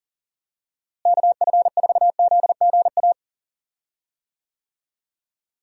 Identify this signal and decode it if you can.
Morse code, heard as a keyed tone, about 37 wpm, 700 Hz. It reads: KF4ZGA
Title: USA Call Signs - 6 Characters - Repeated 3x - 40wpm